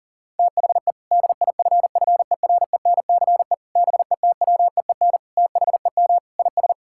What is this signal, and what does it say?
Morse code, heard as a keyed tone, about 40 wpm, 700 Hz. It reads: THE DIFFERENCE BETWEEN THEM IS
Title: Sentences from Top 1600 Words 40wpm